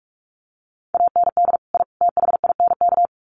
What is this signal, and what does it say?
Morse code, heard as a keyed tone, about 44 wpm, 700 Hz. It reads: AND I THINK
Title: Sets of 3 Words 45wpm